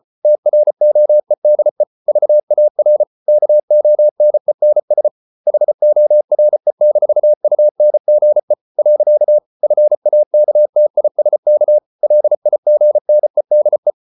Morse code, heard as a keyed tone, about 34 wpm, 600 Hz. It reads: TROEDE VAR KONENS HORE-UNGE. FAKTISK LIGNEDE
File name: Ferie QRQ 34wpm